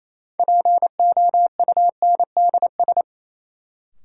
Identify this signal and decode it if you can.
Morse code, heard as a keyed tone, about 27 wpm, 700 Hz. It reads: POUNDS
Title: Sentences from Top 1000 Words - Review Part 2 - New Word Drill - 28wpm